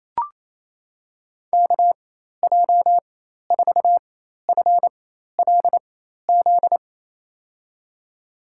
Morse code, heard as a keyed tone, about 28 wpm, spaced as for 13 wpm, 700 Hz. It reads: KJ4FLZ
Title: USA Call Signs - 4x Character Spacing - 28wpm